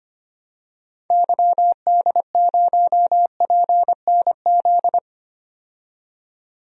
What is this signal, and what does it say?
Morse code, heard as a keyed tone, about 25 wpm, 700 Hz. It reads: YD0PNZ
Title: DXCC-100 Call Signs - 6 Characters - 25wpm